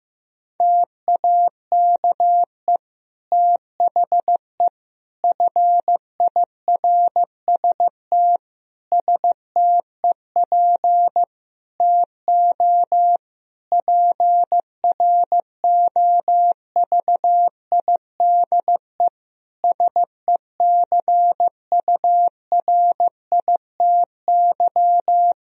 Morse code, heard as a keyed tone, about 15 wpm, 700 Hz. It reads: TAKE THE FIRST STEP TO PROVIDE SECURITY
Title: Sentences from Top 1000 Words - Review Part 4 - 15wpm